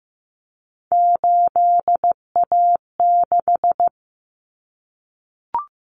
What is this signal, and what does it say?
Morse code, heard as a keyed tone, about 15 wpm, 700 Hz. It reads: 8A6
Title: Common Cut Numbers - 3-Digit 15wpm